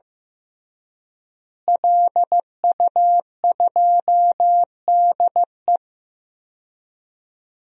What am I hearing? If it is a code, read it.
Morse code, heard as a keyed tone, about 15 wpm, 700 Hz. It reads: LU2DE
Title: DXCC-100 Call Signs - 5 Characters - Repeated 3x - 15wpm